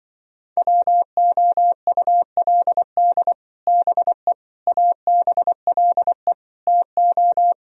Morse code, heard as a keyed tone, about 24 wpm, 700 Hz. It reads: WOULD BE ABLE TO
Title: Speed-Racing - Sets of 4 Words 20wpm